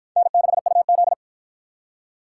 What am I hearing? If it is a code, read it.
Morse code, heard as a keyed tone, about 51 wpm, 700 Hz. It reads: N6UZ